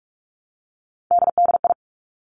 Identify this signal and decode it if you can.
Morse code, heard as a keyed tone, about 46 wpm, 700 Hz. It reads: DDI
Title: Three Challenging Letter-Number Combinations 45wpm